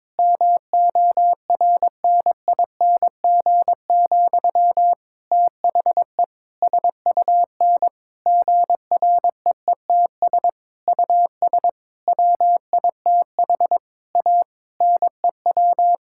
Morse code, heard as a keyed tone, about 22 wpm, 700 Hz. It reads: MORNING, THE SUN GREETS US WITH A NEW
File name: Difficult Sentences with Hints - 22wpm